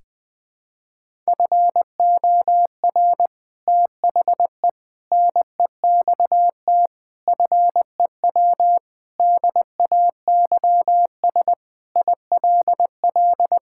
Morse code, heard as a keyed tone, about 20 wpm, 700 Hz. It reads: FOR THE NEXT FEW DAYS ILL
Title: Sentences with 3-Word Phrases - 20wpm